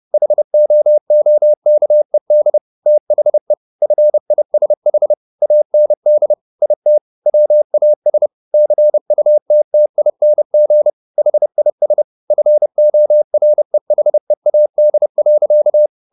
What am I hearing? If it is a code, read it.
Morse code, heard as a keyed tone, about 30 wpm, 600 Hz. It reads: HOOKED THE FISH AND IT WAS CUTTING HIS FOREHEAD.
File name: the old man and the sea / the old man and the sea 1 30wpm